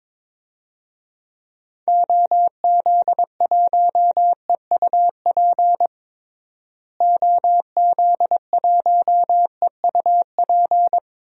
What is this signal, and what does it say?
Morse code, heard as a keyed tone, about 22 wpm, 700 Hz. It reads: OZ1EUP OZ1EUP
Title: DXCC-100 Call Signs - 6 Characters - Repeated 2x - 22wpm